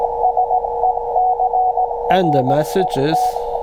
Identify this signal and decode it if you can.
Morse code, heard as a keyed tone, about 17 wpm, 740 Hz, with speech and radio noise. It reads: SECRET